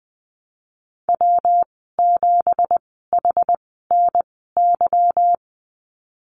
Morse code, heard as a keyed tone, about 19 wpm, spaced as for 14 wpm, 700 Hz. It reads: W7HNY
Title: 186 Lesson 30g - Letter b - Call Signs 20wpm